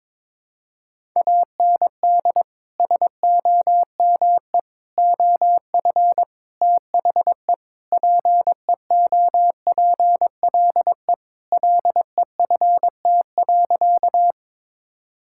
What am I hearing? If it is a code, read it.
Morse code, heard as a keyed tone, about 22 wpm, 700 Hz. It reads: AND SOME OF THE PEOPLE LEFT.
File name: practice / 22wpm / Sentences with 4-Word Phrases - 22wpm